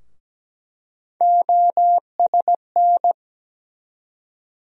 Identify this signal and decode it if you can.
Morse code, heard as a keyed tone, about 17 wpm, 700 Hz. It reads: OSN
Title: USA Call Sign Postfixes - 3 Letters 17wpm